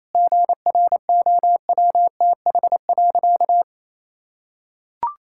Morse code, heard as a keyed tone, about 28 wpm, 700 Hz. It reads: GROWTH.
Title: Sentences from Top 1000 Words - Review Part 5 - 28wpm